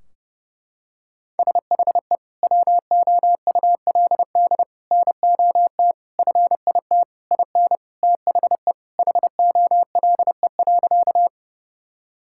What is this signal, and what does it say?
Morse code, heard as a keyed tone, about 30 wpm, 700 Hz. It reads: SHE WOULD NOT FIT IN THE HOLE.
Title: Sentences from Top 1600 Words 30wpm